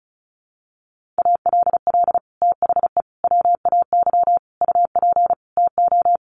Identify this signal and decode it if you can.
Morse code, heard as a keyed tone, about 34 wpm, 700 Hz. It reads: ALL THE WAY UP TO
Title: Sets of 5 Words Encore 35wpm